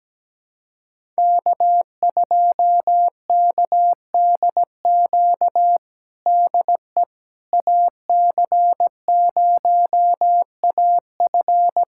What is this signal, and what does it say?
Morse code, heard as a keyed tone, about 17 wpm, 700 Hz. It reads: K2KDQ DE AC0AF
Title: USA Callsign DE Callsign - 5 Characters - 17wpm